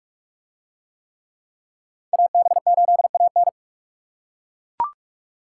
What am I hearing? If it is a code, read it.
Morse code, heard as a keyed tone, about 42 wpm, 700 Hz. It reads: AB8AN